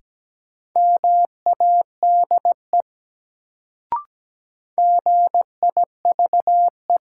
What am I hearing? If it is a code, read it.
Morse code, heard as a keyed tone, about 17 wpm, 700 Hz. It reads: MADE GIVE
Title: Sentences from Top 200 Words - New Word Drill - 17wpm